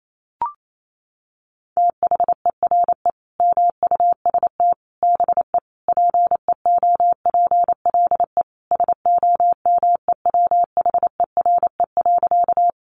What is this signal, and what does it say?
Morse code, heard as a keyed tone, about 28 wpm, 700 Hz. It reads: THERE MUST BE PEOPLE SOMEWHERE.
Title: Sentences from Top 1400 Words 28wpm